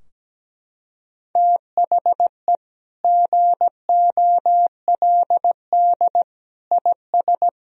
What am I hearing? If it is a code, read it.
Morse code, heard as a keyed tone, about 17 wpm, 700 Hz. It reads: THE GOLD IS
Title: Sentences from Top 1000 Words - Review Part 3 - 17wpm